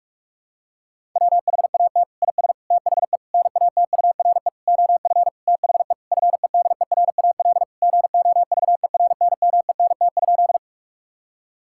Morse code, heard as a keyed tone, about 41 wpm, 700 Hz. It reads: WHAT IS THE NATURE OF THE FEDERAL GOVERNMENT?
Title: Sentences from Top 1000 Words - Review Part 5 - 45wpm